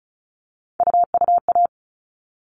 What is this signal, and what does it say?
Morse code, heard as a keyed tone, about 34 wpm, 700 Hz. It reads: UUA
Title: All Cut Numbers - 3-Digit 35wpm